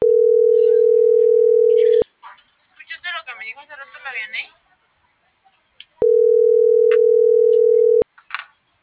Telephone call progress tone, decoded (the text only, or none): ringback tone